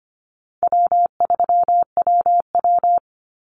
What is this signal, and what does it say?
Morse code, heard as a keyed tone, about 25 wpm, 700 Hz. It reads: W3WW
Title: Easy Call Signs Repeated 2x - 25wpm